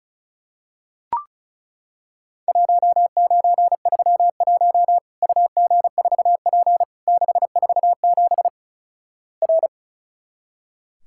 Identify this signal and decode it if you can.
Morse code, heard as a keyed tone, about 36 wpm, 700 Hz. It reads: 1931 UG4P 647 R
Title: Worked All Europe DX Contest - QTC Exchange - 35wpm